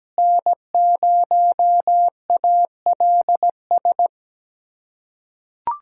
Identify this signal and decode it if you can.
Morse code, heard as a keyed tone, about 17 wpm, 700 Hz. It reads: N0ALS
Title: USA Callsign DE Callsign - 5 Characters - 17wpm